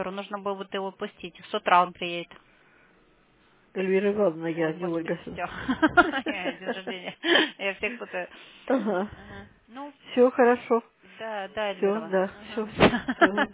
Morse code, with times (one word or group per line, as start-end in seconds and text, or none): none